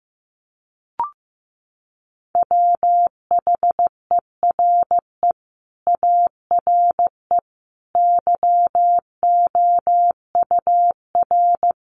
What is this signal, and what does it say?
Morse code, heard as a keyed tone, about 15 wpm, 700 Hz. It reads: WHERE ARE YOUR
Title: Sentences from Top 1200 Words 15wpm